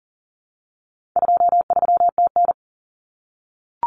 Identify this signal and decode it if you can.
Morse code, heard as a keyed tone, about 40 wpm, 700 Hz. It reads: 23TN
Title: Common Cut Numbers - 4-Digit 40wpm